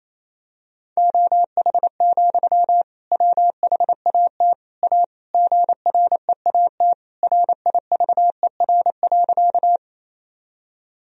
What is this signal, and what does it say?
Morse code, heard as a keyed tone, about 28 wpm, 700 Hz. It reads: OH, WHAT A GREAT RIVER.
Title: Sentences from Top 1000 Words - Review Part 1 - 28wpm